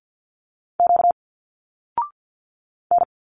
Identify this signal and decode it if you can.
Morse code, heard as a keyed tone, about 50 wpm, 700 Hz. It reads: Q N
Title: Single Letters 50wpm